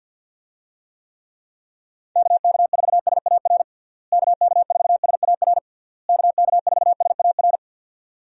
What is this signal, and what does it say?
Morse code, heard as a keyed tone, about 52 wpm, 700 Hz. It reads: KK4SAR KK4SAR KK4SAR
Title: USA Call Signs - 6 Characters - Repeated 3x - 50wpm